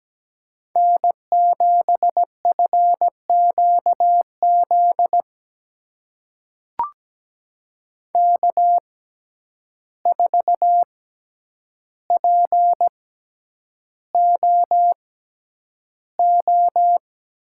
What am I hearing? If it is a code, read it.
Morse code, heard as a keyed tone, about 17 wpm, spaced as for 9 wpm, 700 Hz. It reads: N7FQZ K4POO
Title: USA Call Signs - 5 Characters - 6x Character Spacing - 17wpm